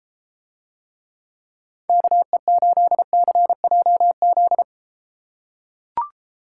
Morse code, heard as a keyed tone, about 33 wpm, 700 Hz. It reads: KE8CJZ